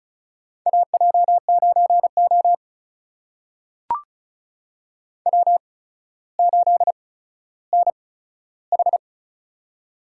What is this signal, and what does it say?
Morse code, heard as a keyed tone, about 34 wpm, spaced as for 15 wpm, 700 Hz. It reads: AJ9O W8NH
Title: USA Call Signs - 4 Characters - 8x Character Spacing - 35wpm